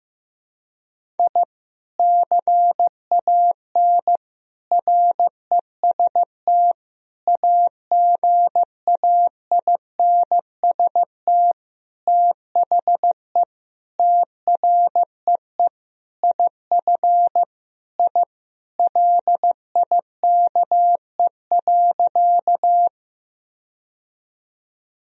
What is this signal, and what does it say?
Morse code, heard as a keyed tone, about 15 wpm, 700 Hz. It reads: I CAN REST AGAINST THE TREE IF I LIKE.
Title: Sentences from Top 1000 Words - Review Part 1 - 15wpm